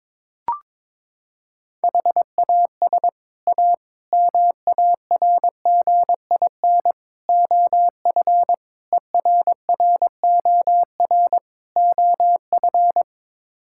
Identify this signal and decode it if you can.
Morse code, heard as a keyed tone, about 22 wpm, 700 Hz. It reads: HAS A MARGIN OF ERROR OF
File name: practice / 22wpm / Sets of 6 Words - 22wpm